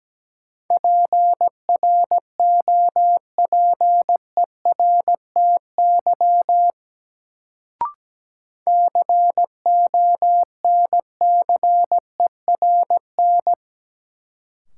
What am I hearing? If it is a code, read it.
Morse code, heard as a keyed tone, about 17 wpm, 700 Hz. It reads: PROPERTY CONCERN
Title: Sentences from Top 1000 Words - New Word Drill - 17wpm